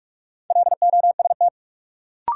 Morse code, heard as a keyed tone, about 46 wpm, 700 Hz. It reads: LOST
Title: Sentences from Top 1400 Words - New Word Drill - 45wpm